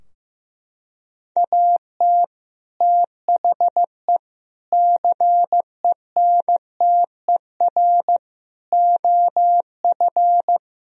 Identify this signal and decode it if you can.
Morse code, heard as a keyed tone, about 16 wpm, 700 Hz. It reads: AT THE CENTER OF
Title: Sentences from Top 1000 Words - Review Part 2 - 15wpm